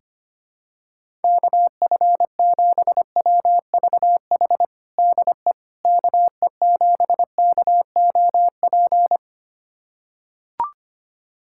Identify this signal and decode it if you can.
Morse code, heard as a keyed tone, about 25 wpm, 700 Hz. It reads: KF7WVH DE KE7KOP
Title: USA Callsign DE Callsign - 6 Characters - 25wpm